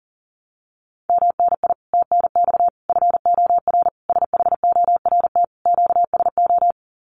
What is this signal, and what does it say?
Morse code, heard as a keyed tone, about 40 wpm, 700 Hz. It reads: MNI TNX FOR SHORT QSO
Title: Common QSO Phrases 40wpm